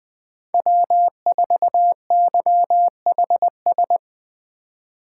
Morse code, heard as a keyed tone, about 21 wpm, 700 Hz. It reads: W4YHS
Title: USA Call Signs - 2x Character Spacing - 20wpm